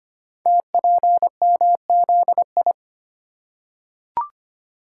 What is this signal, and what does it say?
Morse code, heard as a keyed tone, about 25 wpm, 700 Hz. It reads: TPMZI
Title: Code Groups - 5 Letters - 25wpm